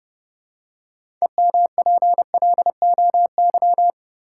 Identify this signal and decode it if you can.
Morse code, heard as a keyed tone, about 30 wpm, 700 Hz. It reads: EMPLOY
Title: Six Letter Words - 30wpm